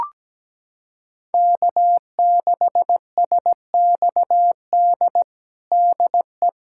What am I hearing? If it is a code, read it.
Morse code, heard as a keyed tone, about 17 wpm, 700 Hz. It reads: K6SXD DE
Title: USA Callsign DE Callsign - 5 Characters - 17wpm